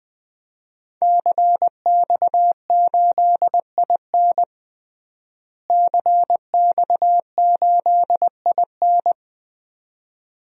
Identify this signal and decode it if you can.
Morse code, heard as a keyed tone, about 21 wpm, 700 Hz. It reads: CX8IN CX8IN
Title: DXCC-100 Call Signs - 5 Characters - Repeated 2x - 20wpm